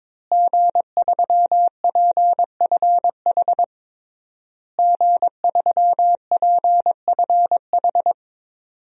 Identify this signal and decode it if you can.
Morse code, heard as a keyed tone, about 22 wpm, 700 Hz. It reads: G3PFH G3PFH
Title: DXCC-100 Call Signs - 5 Characters - Repeated 3x - 22wpm